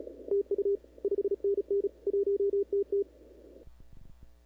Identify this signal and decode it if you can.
Morse code, heard as a keyed tone, about 36 wpm, 400 Hz. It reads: TU 5NN 1TT
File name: CW / 2005-k5tr arrl cw dx -17:00